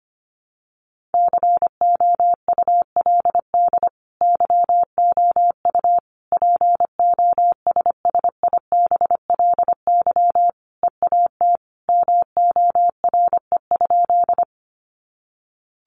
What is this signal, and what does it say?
Morse code, heard as a keyed tone, about 25 wpm, 700 Hz. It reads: COULD YOU POSSIBLY EAT MORE?
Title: Sentences from Top 1300 Words 25wpm